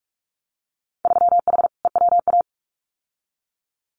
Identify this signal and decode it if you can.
Morse code, heard as a keyed tone, about 46 wpm, 700 Hz. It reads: 3H EWA
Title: Winter Field Day Exchanges Encore 45wpm